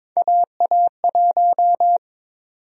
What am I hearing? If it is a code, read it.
Morse code, heard as a keyed tone, about 22 wpm, 700 Hz. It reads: AA1
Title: USA Call Sign Prefixes - 22wpm